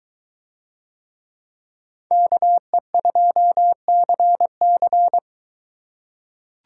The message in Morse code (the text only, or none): KE2CC